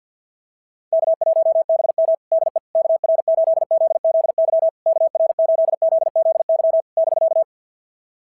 Morse code, heard as a keyed tone, about 46 wpm, 650 Hz. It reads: K1BM DE KR8ZZY KR8ZZY <BK>
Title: SOTA QSO Practice 50wpm